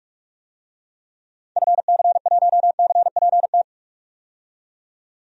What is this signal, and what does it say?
Morse code, heard as a keyed tone, about 47 wpm, 700 Hz. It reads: FC1CPT
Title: DXCC-100 Call Signs - 6 Characters - 45wpm